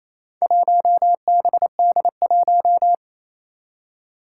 Morse code, heard as a keyed tone, about 28 wpm, 700 Hz. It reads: 1BD1
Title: Four Challenging Letter-Number Combinations 28wpm